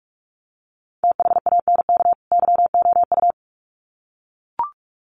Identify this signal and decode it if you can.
Morse code, heard as a keyed tone, about 44 wpm, 700 Hz. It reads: THANK YOU
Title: Sets of 2 Words 45wpm